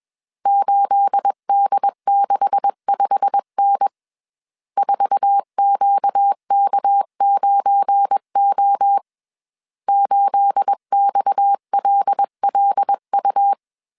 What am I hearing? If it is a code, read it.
Morse code, heard as a keyed tone, about 21 wpm, 800 Hz. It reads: 8D65N 4QK9O 8XLLU